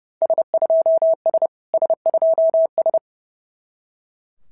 Morse code, heard as a keyed tone, about 28 wpm, 660 Hz. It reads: S2S S2S
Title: SOTA QSO Practice 30wpm